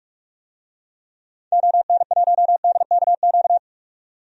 Morse code, heard as a keyed tone, about 46 wpm, 700 Hz. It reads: ON1DKQ
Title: DXCC-100 Call Signs - 6 Characters - 45wpm